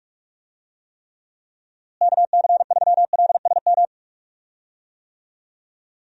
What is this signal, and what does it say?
Morse code, heard as a keyed tone, about 44 wpm, 700 Hz. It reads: KC3LSM
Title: USA Call Signs - 6 Characters - Repeated 2x - 45wpm